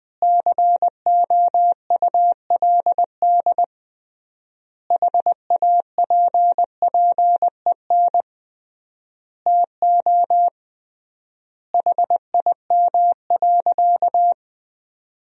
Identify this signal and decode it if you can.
Morse code, heard as a keyed tone, about 21 wpm, 700 Hz. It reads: COULD HAPPEN TO HIM.
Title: 3x Word-Spacing - Sentences from Top 400 Words - 20wpm